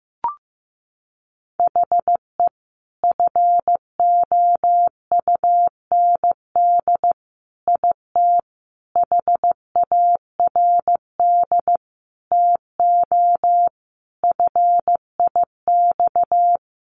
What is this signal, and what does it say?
Morse code, heard as a keyed tone, about 15 wpm, 700 Hz. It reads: HE FOUND IT HARD TO FIX
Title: Sentences from Top 1400 Words 15wpm